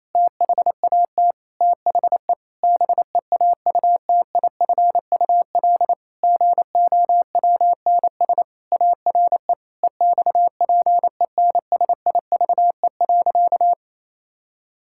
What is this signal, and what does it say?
Morse code, heard as a keyed tone, about 28 wpm, 700 Hz. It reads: THAT THE BEAUTIFUL GOWNS ARE EXPENSIVE.
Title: Pangram Challenge v2 - 28wpm